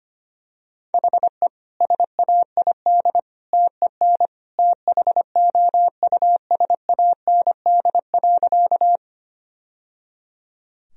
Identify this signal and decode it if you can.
Morse code, heard as a keyed tone, about 25 wpm, 700 Hz. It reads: HE SAID TEN THOUSAND.
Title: Sentences from Top 1000 Words - Review Part 3 - 25wpm